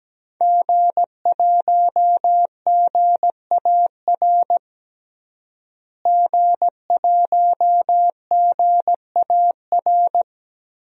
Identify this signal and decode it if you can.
Morse code, heard as a keyed tone, about 17 wpm, 700 Hz. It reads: G1GAR G1GAR
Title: DXCC-100 Call Signs - 5 Characters - Repeated 2x - 17wpm